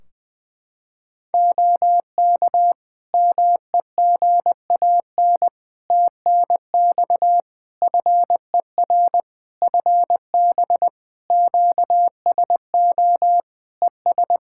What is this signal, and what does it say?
Morse code, heard as a keyed tone, about 21 wpm, 700 Hz. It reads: OK MEGAN TNX FER FB QSO ES
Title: Practice QSO - Exchange 4 - Ending - 20wpm